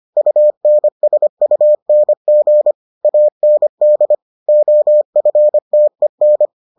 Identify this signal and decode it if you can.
Morse code, heard as a keyed tone, about 25 wpm, 600 Hz. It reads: UNSUNG AND OFTEN